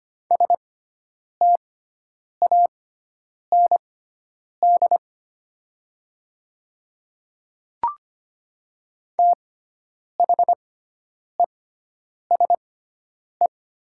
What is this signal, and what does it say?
Morse code, heard as a keyed tone, about 25 wpm, spaced as for 9 wpm, 700 Hz. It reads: STAND THESE
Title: 6x Character Spacing - Top 300 Words - 25wpm